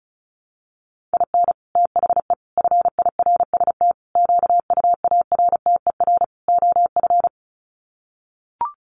Morse code, heard as a keyed tone, about 34 wpm, 700 Hz. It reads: IN THE FIRST QUARTER OF
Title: Sets of 5 Words Encore2 35wpm